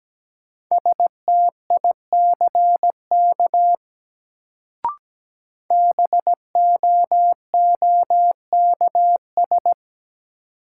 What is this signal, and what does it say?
Morse code, heard as a keyed tone, about 17 wpm, 700 Hz. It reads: STICK BOOKS
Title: Sentences from Top 1200 Words - New Word Drill Alt - 17wpm